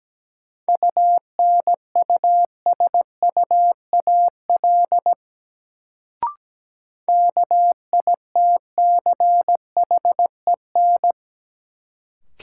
Morse code, heard as a keyed tone, about 17 wpm, 700 Hz. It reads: UNUSUAL KITCHEN
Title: Sentences from Top 1400 Words - New Word Drill Alt - 17wpm